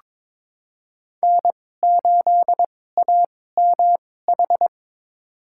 Morse code, heard as a keyed tone, about 22 wpm, spaced as for 16 wpm, 700 Hz. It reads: N8AMH
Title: USA Call Signs - 5 Characters - 2x Character Spacing - 22wpm